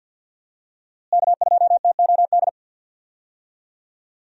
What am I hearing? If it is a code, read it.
Morse code, heard as a keyed tone, about 53 wpm, 700 Hz. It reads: K2TOD
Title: USA Call Signs - 5 Characters - Repeated 2x - 50wpm